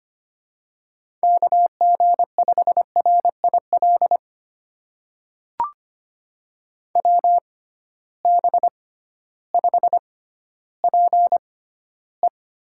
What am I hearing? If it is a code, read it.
Morse code, heard as a keyed tone, about 25 wpm, spaced as for 14 wpm, 700 Hz. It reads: KG5RIL WB5PE
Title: USA Call Signs - 6 Characters - 6x Character Spacing - 25wpm